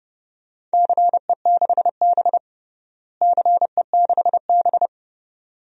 Morse code, heard as a keyed tone, about 30 wpm, 700 Hz. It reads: CE6B CE6B
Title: DXCC-100 Call Signs - 4 Characters - Repeated 2x - 30wpm